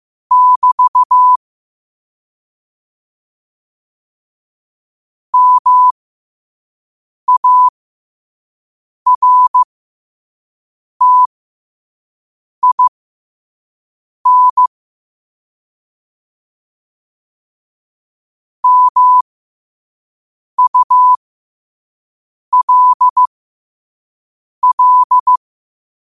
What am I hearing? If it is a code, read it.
Morse code, heard as a keyed tone, about 15 wpm, spaced as for 5 wpm, 1000 Hz. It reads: <BT> MARTIN MULL
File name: qotdfiles / 5wpm